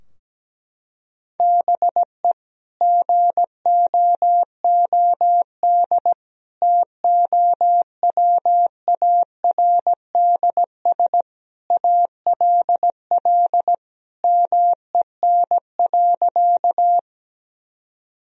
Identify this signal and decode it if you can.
Morse code, heard as a keyed tone, about 17 wpm, 700 Hz. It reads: BE GOOD TOWARDS ALL MEN.
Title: Sentences from Top 1000 Words - Review Part 2 - 17wpm